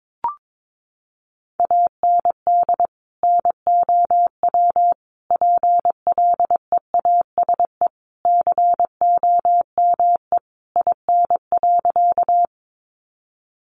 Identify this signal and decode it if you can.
Morse code, heard as a keyed tone, about 22 wpm, 700 Hz. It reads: AND NOW PLEASE COME IN.
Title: Sentences from Top 1500 Words 22wpm